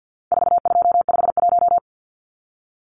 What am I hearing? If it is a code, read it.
Morse code, heard as a keyed tone, about 48 wpm, 700 Hz. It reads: 4251
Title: Numbers - 4-Digit 50wpm